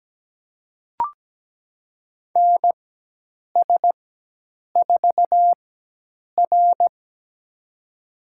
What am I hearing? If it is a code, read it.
Morse code, heard as a keyed tone, about 17 wpm, spaced as for 8 wpm, 700 Hz. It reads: NS4R
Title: USA Call Signs - 4x Character Spacing - 17wpm